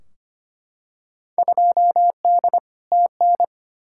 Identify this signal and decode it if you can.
Morse code, heard as a keyed tone, about 25 wpm, 700 Hz. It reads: 2D TN
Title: Field Day Exchanges Encore - 25wpm